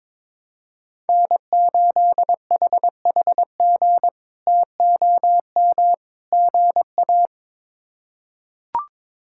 Morse code, heard as a keyed tone, about 22 wpm, 700 Hz. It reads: N8HHG TOM GA
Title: K1USN SST QSO Practice - 22wpm